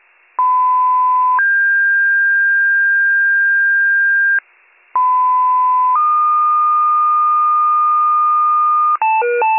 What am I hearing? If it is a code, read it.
Two-tone page, heard as a one-second tone, then a three-second tone, about 1010 then 1600 Hz, about 1010 then 1220 Hz, with noise. It is heard from 0.4 to 9.0 s.